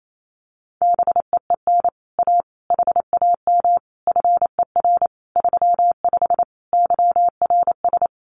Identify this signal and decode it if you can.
Morse code, heard as a keyed tone, about 28 wpm, 700 Hz. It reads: BEEN A HAM FER 35 YRS
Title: Common QSO Phrases Encore 28wpm